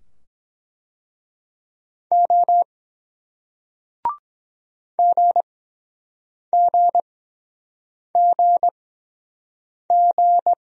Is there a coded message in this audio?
Yes: Morse code from 2.1 to 10.5 s, heard as a keyed tone, about 23 wpm, spaced as for 7 wpm, 700 Hz. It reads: O GGGG